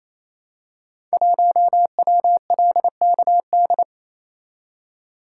Morse code, heard as a keyed tone, about 29 wpm, 700 Hz. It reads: 1WLKD
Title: Code Groups - 5 Letter:Numbers - 28wpm